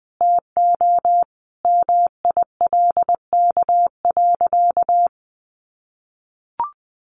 Morse code, heard as a keyed tone, about 20 wpm, 700 Hz. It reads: TO MILK.
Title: Sentences from Top 1500 Words 20wpm